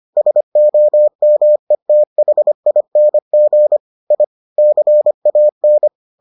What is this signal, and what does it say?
Morse code, heard as a keyed tone, about 25 wpm, 600 Hz. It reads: SOMETHING I CAN